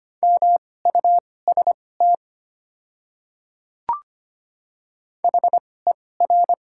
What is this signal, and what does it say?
Morse code, heard as a keyed tone, about 25 wpm, spaced as for 18 wpm, 700 Hz. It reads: MUST HER
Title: 2x Character Spacing - Top 300 Words - 25wpm